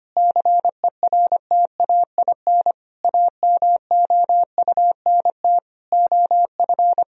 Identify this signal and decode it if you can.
Morse code, heard as a keyed tone, about 25 wpm, 700 Hz. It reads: CERTAIN AMOUNT OF
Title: Sentences from Top 1000 Words - Review Part 4 - 25wpm